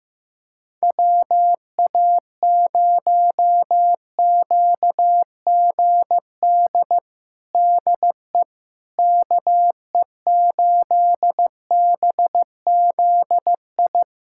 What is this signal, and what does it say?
Morse code, heard as a keyed tone, about 16 wpm, 700 Hz. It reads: WA0QGD DE KE8BZI
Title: USA Callsign DE Callsign 15wpm